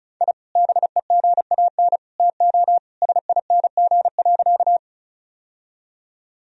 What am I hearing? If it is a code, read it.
Morse code, heard as a keyed tone, about 34 wpm, 700 Hz. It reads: I BEGAN TO SING.
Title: Sentences from Top 1000 Words - Review Part 2 - 35wpm